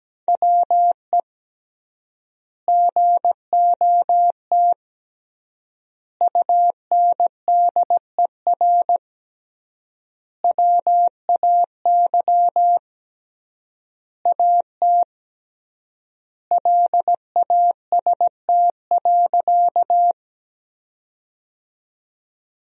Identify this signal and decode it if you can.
Morse code, heard as a keyed tone, about 17 wpm, 700 Hz. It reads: WE GOT UNDER WAY AT LAST.
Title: 3x Word-Spacing - Sentences from Top 300 Words - 17wpm